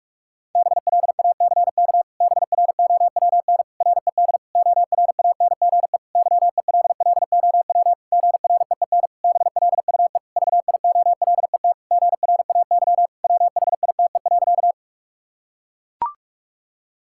Morse code, heard as a keyed tone, about 41 wpm, 700 Hz. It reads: BLACK BROWN RED ORANGE YELLOW GREEN BLUE VIOLET GRAY WHITE.
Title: Pangram Challenge v2 - 45wpm